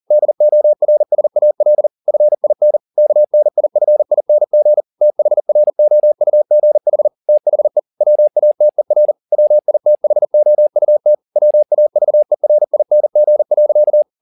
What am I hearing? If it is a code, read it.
Morse code, heard as a keyed tone, about 40 wpm, 600 Hz. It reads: DORSAL FIN KNIFING THROUGH THE WATER WITHOUT WAVERING.
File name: the old man and the sea 2 40wpm